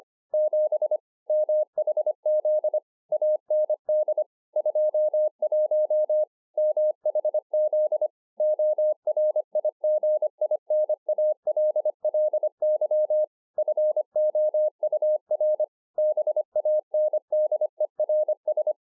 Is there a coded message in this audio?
Yes: Morse code from 0.3 to 18.7 s, heard as a keyed tone, about 24 wpm, 610 Hz. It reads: 7 MHZ AND 21 MHZ ORIGINALLY FOUR BANDERS